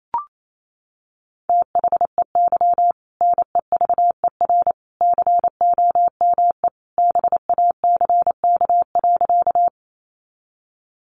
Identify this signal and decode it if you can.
Morse code, heard as a keyed tone, about 28 wpm, 700 Hz. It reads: THEY NEVER COME BACK.
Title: Sentences from Top 300 Words 28wpm